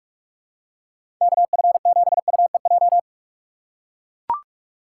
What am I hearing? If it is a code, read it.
Morse code, heard as a keyed tone, about 43 wpm, 700 Hz. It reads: KF7UEJ
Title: USA Call Signs - 6 Characters - 4x Character Spacing - 45wpm